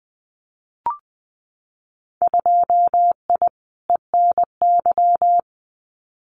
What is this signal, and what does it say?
Morse code, heard as a keyed tone, about 20 wpm, 700 Hz. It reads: 2I ENY
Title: Winter Field Day Exchanges Encore 20wpm